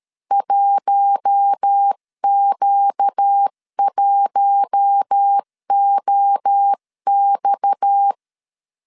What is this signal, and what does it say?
Morse code, heard as a keyed tone, about 13 wpm, 800 Hz. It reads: 1Q1OX
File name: CW / DET / 1 6 1500C S C 12wpm 3533